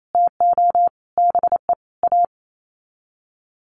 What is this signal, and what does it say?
Morse code, heard as a keyed tone, about 28 wpm, 700 Hz. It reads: TO BE A
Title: Sets of 5 Words Encore 28wpm